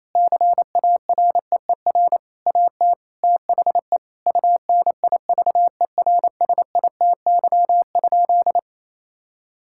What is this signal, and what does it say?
Morse code, heard as a keyed tone, about 28 wpm, 700 Hz. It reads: CAREER AT THE UNIVERSITY?
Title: Sentences from Top 1000 Words - Review Part 6 - 28wpm